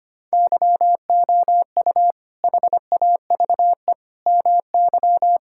Morse code, heard as a keyed tone, about 25 wpm, 700 Hz. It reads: YOU HAVE MY
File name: Sentences with 2-Word Phrases - 25wpm